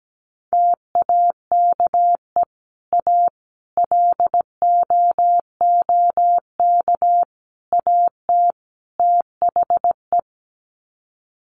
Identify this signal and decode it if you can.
Morse code, heard as a keyed tone, about 17 wpm, 700 Hz. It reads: TAKE A LOOK AT THE
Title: Sets of 5 Words 17wpm